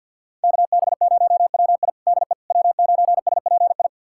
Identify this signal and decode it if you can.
Morse code, heard as a keyed tone, about 53 wpm, 700 Hz. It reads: KB0WI DE W9SPI
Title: USA Callsign DE Callsign - 5 Characters - 50wpm